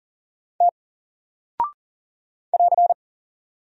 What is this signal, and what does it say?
Morse code, heard as a keyed tone, about 39 wpm, 700 Hz. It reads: T <AR>
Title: Single Character (Letter:Number:Punctuation:Prosign) 40wpm